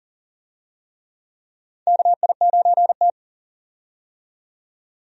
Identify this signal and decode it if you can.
Morse code, heard as a keyed tone, about 40 wpm, 700 Hz. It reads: KI9T
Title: USA Call Signs - 4 Characters - 40wpm